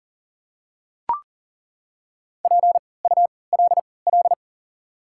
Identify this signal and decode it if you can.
Morse code, heard as a keyed tone, about 40 wpm, spaced as for 23 wpm, 700 Hz. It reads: PULL
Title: 3x Character Spacing - Top 500 Words - 40wpm